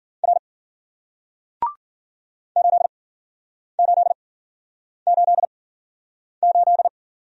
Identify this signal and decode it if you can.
Morse code, heard as a keyed tone, about 53 wpm, spaced as for 9 wpm, 700 Hz. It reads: R 8888